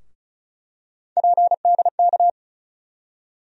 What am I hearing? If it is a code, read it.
Morse code, heard as a keyed tone, about 36 wpm, 700 Hz. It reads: PDK
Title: USA Call Sign Postfixes - 3 Letters 35wpm